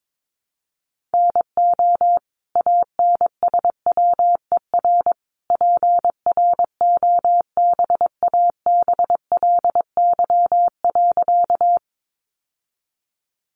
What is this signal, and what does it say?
Morse code, heard as a keyed tone, about 22 wpm, 700 Hz. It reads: NO ANSWER PROBABLY.
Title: Sentences from Top 700 Words 22wpm